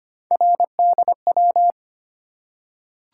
Morse code, heard as a keyed tone, about 25 wpm, 700 Hz. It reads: RDW
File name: Three Letter-Number Combinations - 25wpm